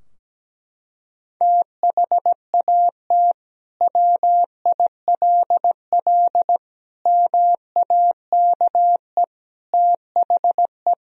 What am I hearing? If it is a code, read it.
Morse code, heard as a keyed tone, about 17 wpm, 700 Hz. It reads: THAT WILL MAKE THE
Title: Sentences from Top 1000 Words - Review Part 3 - 17wpm